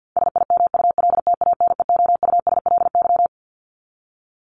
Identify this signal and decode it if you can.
Morse code, heard as a keyed tone, about 48 wpm, 700 Hz. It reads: SIMULTANEOUSLY
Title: LY Suffix - Top 100 Words with Base Words 50wpm